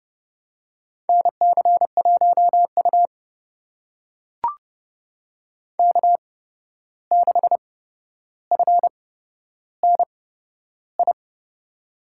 Morse code, heard as a keyed tone, about 30 wpm, spaced as for 12 wpm, 700 Hz. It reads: NC1U K6FNI
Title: USA Call Signs - 8x Character Spacing - 30wpm